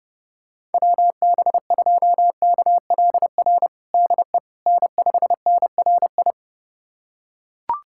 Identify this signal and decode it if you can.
Morse code, heard as a keyed tone, about 30 wpm, 700 Hz. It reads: WB2KLR DE N5NRI
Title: USA Callsign DE Callsign 30wpm